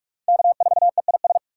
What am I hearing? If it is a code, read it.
Morse code, heard as a keyed tone, about 58 wpm, 700 Hz. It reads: K4EIS